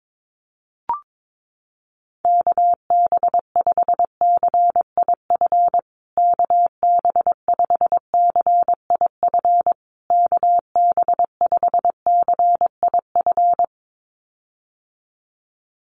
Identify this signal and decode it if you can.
Morse code, heard as a keyed tone, about 22 wpm, 700 Hz. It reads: KB5CIF KB5CIF KB5CIF
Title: USA Call Signs Repeated 3x 22wpm